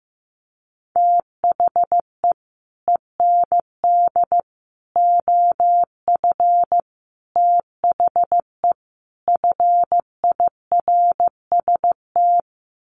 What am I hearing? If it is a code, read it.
Morse code, heard as a keyed tone, about 15 wpm, 700 Hz. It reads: THE END OF THE FIRST
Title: Sets of 5 Words Encore 15wpm